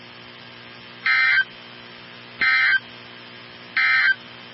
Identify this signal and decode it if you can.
Emergency Alert System (SAME) tones, heard as harsh buzzing bursts. It is heard from 0.9 to 4.2 s.
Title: NOAA Weather Radio 162.550